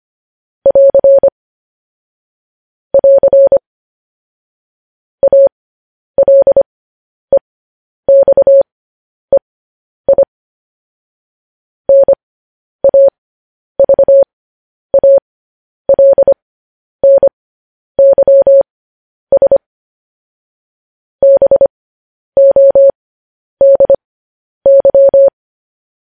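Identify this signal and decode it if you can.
Morse code, heard as a keyed tone, about 25 wpm, spaced as for 10 wpm, 550 Hz. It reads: <AR> <AR> ALEXEI NAVALNYS BODY
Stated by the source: Morse code news headlines practice transmission.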